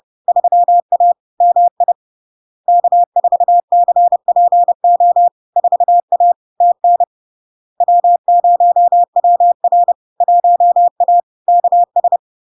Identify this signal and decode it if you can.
Morse code, heard as a keyed tone, about 30 wpm, 700 Hz. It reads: 3A MI K4CPO 4A TN W0WR 1A KS
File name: LARK-FD-2022-Log-in-CW-30wpm-700-Hz